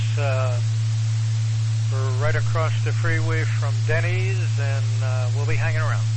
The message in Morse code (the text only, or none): none